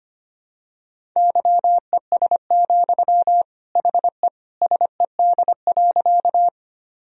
Morse code, heard as a keyed tone, about 25 wpm, 700 Hz. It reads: YES, HE SED.
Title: Sentences from Top 500 Words and Abbreviations 25wpm